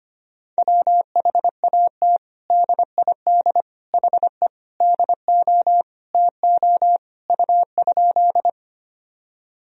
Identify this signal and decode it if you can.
Morse code, heard as a keyed tone, about 25 wpm, 700 Hz. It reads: WHAT DID HE DO TO U?
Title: Sentences from Top 100 Words and Abbreviations 25wpm